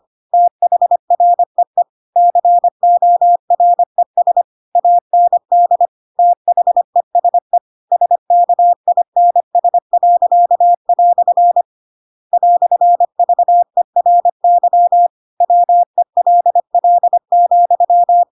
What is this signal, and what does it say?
Morse code, heard as a keyed tone, about 25 wpm, 700 Hz. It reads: THREE CORES AND THESE SKINS." "VERY WELL,
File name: Pinocchio / Pinocchio ch7 25 25 ik1hge